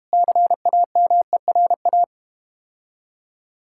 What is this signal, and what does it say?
Morse code, heard as a keyed tone, about 32 wpm, 690 Hz. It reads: CAMERA